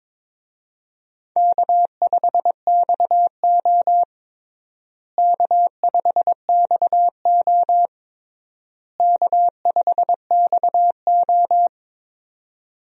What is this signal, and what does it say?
Morse code, heard as a keyed tone, about 22 wpm, 700 Hz. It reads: K5XO K5XO K5XO
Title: USA Call Signs - 4 Characters - Repeated 3x - 22wpm